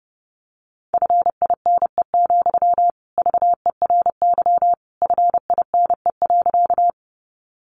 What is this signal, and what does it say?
Morse code, heard as a keyed tone, about 30 wpm, 700 Hz. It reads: FINE, VERY FINE.
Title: Sentences from Top 500 Words 30wpm